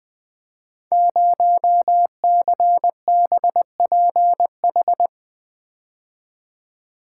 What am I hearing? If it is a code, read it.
Morse code, heard as a keyed tone, about 20 wpm, 700 Hz. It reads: 0CBPH
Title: Code Groups - 5 Letter:Numbers - 20wpm